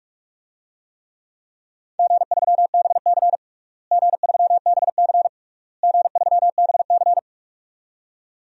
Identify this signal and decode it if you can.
Morse code, heard as a keyed tone, about 45 wpm, 700 Hz. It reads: G3BC G3BC G3BC
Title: DXCC-100 Call Signs - 4 Characters - Repeated 3x - 45wpm